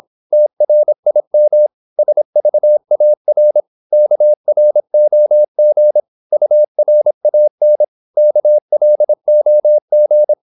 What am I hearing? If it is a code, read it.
Morse code, heard as a keyed tone, about 26 wpm, 600 Hz. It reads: TRIM SVAR KROG URAN KLOG
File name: Korte ord DK 26wpm